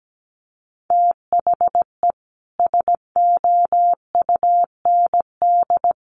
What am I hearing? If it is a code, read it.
Morse code, heard as a keyed tone, about 17 wpm, 700 Hz. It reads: THE SOUND
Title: Sets of 3 Words 17wpm